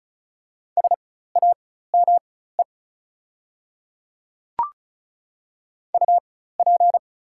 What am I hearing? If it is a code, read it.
Morse code, heard as a keyed tone, about 34 wpm, spaced as for 16 wpm, 690 Hz. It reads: SAME UP
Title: 4x Character Spacing - Top 200 Words - 35wpm